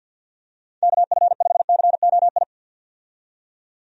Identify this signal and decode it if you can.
Morse code, heard as a keyed tone, about 47 wpm, 700 Hz. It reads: KF5COI
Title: USA Call Signs - 6 Characters - 50wpm